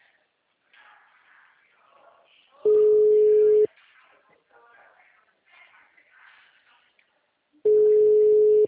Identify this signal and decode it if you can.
Ringback tone (Europe, 425 Hz), with speech.